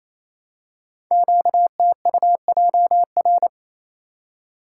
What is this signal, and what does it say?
Morse code, heard as a keyed tone, about 28 wpm, 700 Hz. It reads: QTUJR
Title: Code Groups - 5 Letters - 28wpm